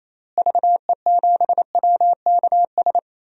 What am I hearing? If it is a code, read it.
Morse code, heard as a keyed tone, about 28 wpm, 700 Hz. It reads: VE7WKS
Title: DXCC-100 Call Signs - 6 Characters - Repeated 3x - 28wpm